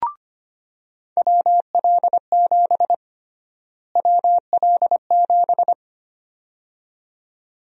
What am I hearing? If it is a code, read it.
Morse code, heard as a keyed tone, about 25 wpm, 700 Hz. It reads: WL7 WL7
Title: USA Call Sign Prefixes - Repeated 2x - 25wpm